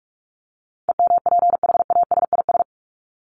Morse code, heard as a keyed tone, about 44 wpm, 700 Hz. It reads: EMPHASIS
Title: Top 1251-1500 Words 45wpm